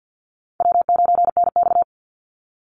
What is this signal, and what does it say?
Morse code, heard as a keyed tone, about 48 wpm, 700 Hz. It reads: W9NX
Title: Easy Call Signs - 50wpm